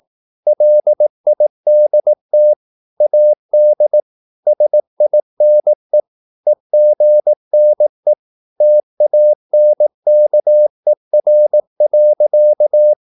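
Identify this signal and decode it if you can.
Morse code, heard as a keyed tone, about 18 wpm, 600 Hz. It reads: LIDT AD SINE EGNE TANKER.
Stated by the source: Slow (QRS) Morse practice at 18 wpm, CW med Gnister.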